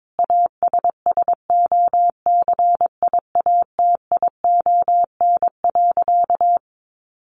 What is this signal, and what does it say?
Morse code, heard as a keyed tone, about 22 wpm, 700 Hz. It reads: ASSOCIATION.
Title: Sentences from Top 1200 Words 22wpm